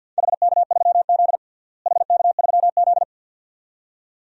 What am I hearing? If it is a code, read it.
Morse code, heard as a keyed tone, about 52 wpm, 700 Hz. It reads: HK3Z HK3Z
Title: DXCC-100 Call Signs - 4 Characters - Repeated 2x - 50wpm